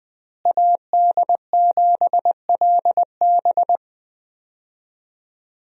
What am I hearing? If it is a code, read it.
Morse code, heard as a keyed tone, about 20 wpm, 700 Hz. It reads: AD7LB
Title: USA Call Signs - 5 Characters - Repeated 3x - 20wpm